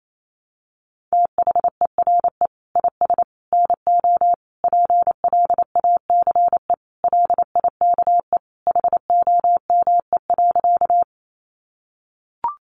Morse code, heard as a keyed tone, about 28 wpm, 700 Hz. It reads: THERE IS NO PLACE LIKE HOME.
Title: Sentences from Top 200 Words 28wpm